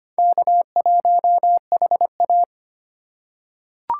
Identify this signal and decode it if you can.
Morse code, heard as a keyed tone, about 25 wpm, 700 Hz. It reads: K1HA
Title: USA Call Signs - 4 Characters - 3x Character Spacing - 25wpm